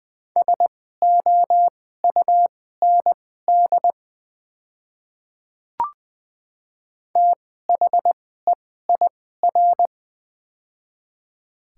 Morse code, heard as a keyed tone, about 20 wpm, spaced as for 15 wpm, 700 Hz. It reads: SOUND THEIR
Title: 2x Character Spacing - Top 300 Words - 20wpm